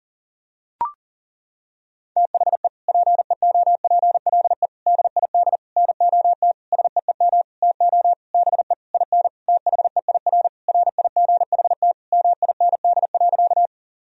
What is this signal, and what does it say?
Morse code, heard as a keyed tone, about 40 wpm, 700 Hz. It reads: THE PEOPLE DID NOT SEEM TO BE IN THEIR RIGHT MIND.
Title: Sentences from Top 1000 Words - Review Part 1 - 40wpm